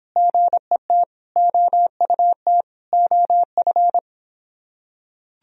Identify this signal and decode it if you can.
Morse code, heard as a keyed tone, about 24 wpm, 700 Hz. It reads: GET OUT OF